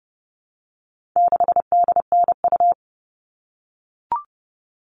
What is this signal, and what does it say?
Morse code, heard as a keyed tone, about 30 wpm, 700 Hz. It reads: 6DNU
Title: All Cut Numbers - 4-Digit 30wpm